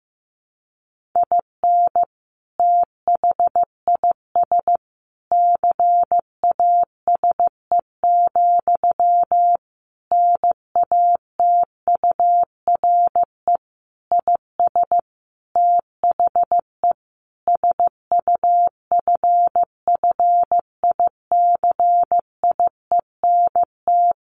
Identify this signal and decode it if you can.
Morse code, heard as a keyed tone, about 15 wpm, 700 Hz. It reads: IN THIS CASE, NATURE IS THE SUFFICIENT
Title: Sentences from Top 1400 Words 15wpm